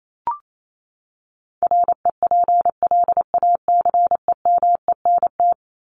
Morse code, heard as a keyed tone, about 28 wpm, 700 Hz. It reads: REPLACEMENT
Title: Top 1751-2000 Words 28wpm